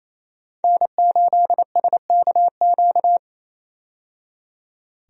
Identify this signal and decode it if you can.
Morse code, heard as a keyed tone, about 28 wpm, 700 Hz. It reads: N8SKQ
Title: USA Call Signs - 5 Characters - 28wpm